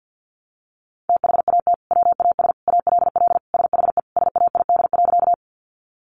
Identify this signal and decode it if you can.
Morse code, heard as a keyed tone, about 50 wpm, 700 Hz. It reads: THAT WAS ALL SHE SAID.
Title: Sentences from Top 100 Words 50wpm